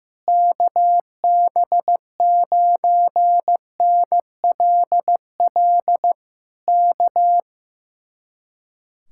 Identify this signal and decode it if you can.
Morse code, heard as a keyed tone, about 15 wpm, 700 Hz. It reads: KB9NLL K
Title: Speed-Racing Common QSO Phrases Encore - 15wpm